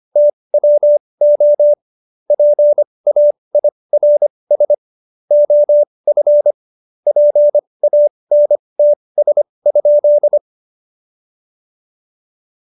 Morse code, heard as a keyed tone, about 25 wpm, spaced as for 20 wpm, 590 Hz. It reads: TWO PAIRS OF PANTS?
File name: Jokes-25-char-20-eff